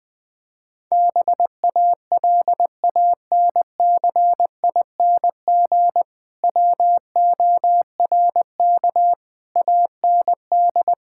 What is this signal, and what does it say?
Morse code, heard as a keyed tone, about 21 wpm, 700 Hz. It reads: BALANCING WORK AND
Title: Difficult Sentences with Hints - 20wpm